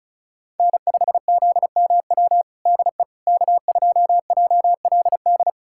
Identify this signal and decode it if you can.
Morse code, heard as a keyed tone, about 36 wpm, 700 Hz. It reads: N5ZMW DE K2JLD
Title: USA Callsign DE Callsign - 5 Characters - 35wpm